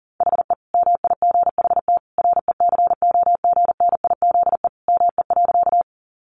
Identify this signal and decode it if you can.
Morse code, heard as a keyed tone, about 40 wpm, 700 Hz. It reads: HE MIGHT RECOGNIZE ME.
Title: Sentences from Top 1400 Words 40wpm